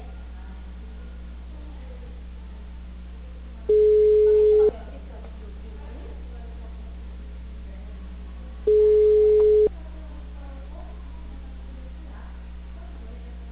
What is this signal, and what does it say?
Ringback tone (Europe, 425 Hz), with speech.